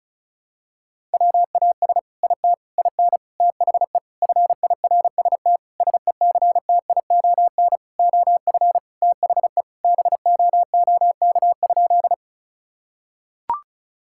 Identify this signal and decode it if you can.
Morse code, heard as a keyed tone, about 34 wpm, 700 Hz. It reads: WAS IT IN THE FIRST SECTION OF THE BOOK?
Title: Sentences from Top 1000 Words - Review Part 5 - 35wpm